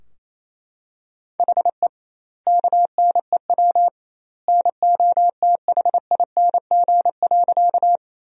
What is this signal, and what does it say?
Morse code, heard as a keyed tone, about 28 wpm, 700 Hz. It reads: HE KNEW NOTHING.
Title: 2x Word-Spacing - Sentences from Top 500 Words - 28wpm